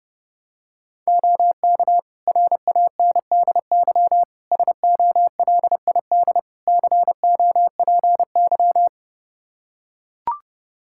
Morse code, heard as a keyed tone, about 30 wpm, 700 Hz. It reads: OK RANDY SOLID COPY
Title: Practice QSO - Exchange 3 - More Info - 30wpm